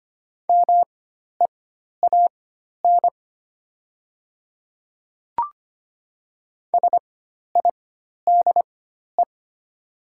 Morse code, heard as a keyed tone, about 25 wpm, spaced as for 12 wpm, 700 Hz. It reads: MEAN SIDE
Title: 4x Character Spacing - Top 200 Words - 25wpm